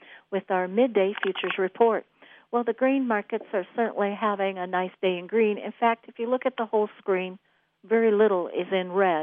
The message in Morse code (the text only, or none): none